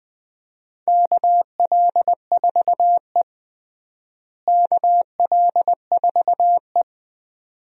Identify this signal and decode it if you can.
Morse code, heard as a keyed tone, about 20 wpm, 700 Hz. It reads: KL4E KL4E
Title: USA Call Signs - 4 Characters - Repeated 2x - 20wpm